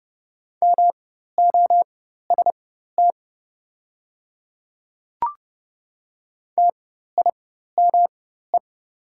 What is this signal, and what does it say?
Morse code, heard as a keyed tone, about 32 wpm, spaced as for 14 wpm, 700 Hz. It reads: MOST TIME
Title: 4x Character Spacing - Top 300 Words - 30wpm